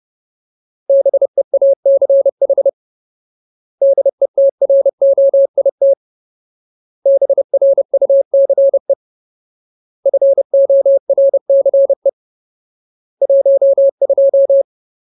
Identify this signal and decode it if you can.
Morse code, heard as a keyed tone, about 30 wpm, 550 Hz. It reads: BEACH DETROIT BRUCE FORCE 12
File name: Ultimate Word List / 30WPM Ultimate